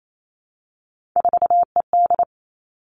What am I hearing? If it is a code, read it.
Morse code, heard as a keyed tone, about 28 wpm, 700 Hz. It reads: 4ED
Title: Three Challenging Letter-Number Combinations 28wpm